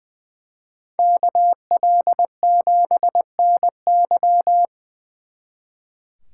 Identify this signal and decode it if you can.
Morse code, heard as a keyed tone, about 25 wpm, 700 Hz. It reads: KL7NY
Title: Speed Racing - USA Call Signs 20wpm